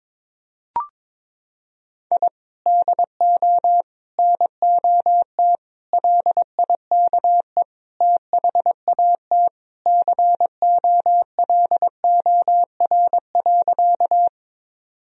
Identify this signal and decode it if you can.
Morse code, heard as a keyed tone, about 22 wpm, 700 Hz. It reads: I DO NOT LIKE THAT COLOR.
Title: Sentences from Top 1000 Words - Review Part 2 - 22wpm